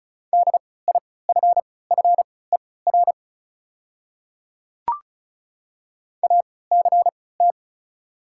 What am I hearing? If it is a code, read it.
Morse code, heard as a keyed tone, about 36 wpm, spaced as for 20 wpm, 700 Hz. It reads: DIFFER ACT
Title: 3x Character Spacing - Top 200 Words - 35wpm